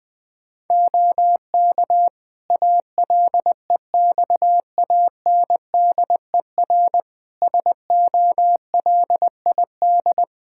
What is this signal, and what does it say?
Morse code, heard as a keyed tone, about 20 wpm, 700 Hz. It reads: OK ALEXANDER SOLID
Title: K1USN SST QSO Practice 20wpm